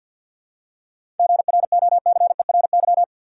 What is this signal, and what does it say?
Morse code, heard as a keyed tone, about 50 wpm, 700 Hz. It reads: GROCERY
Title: Seven Letter Words - 50wpm